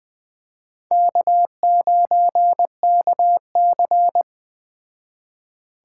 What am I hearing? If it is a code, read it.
Morse code, heard as a keyed tone, about 21 wpm, 700 Hz. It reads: K9KC
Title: Speed Racing - USA 1x2 Call Signs 17wpm